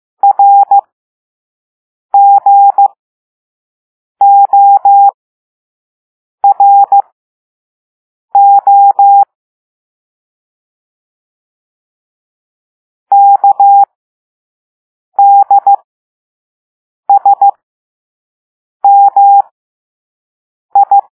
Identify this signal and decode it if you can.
Morse code, heard as a keyed tone, about 15 wpm, spaced as for 5 wpm, 800 Hz. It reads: RGORO KDSMI
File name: Morse / Level6-All-A